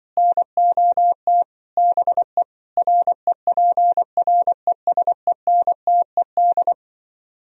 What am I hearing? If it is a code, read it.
Morse code, heard as a keyed tone, about 25 wpm, 700 Hz. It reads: NOT BE REPRESENTED
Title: Speed-Racing - Sets of 5 Words 20wpm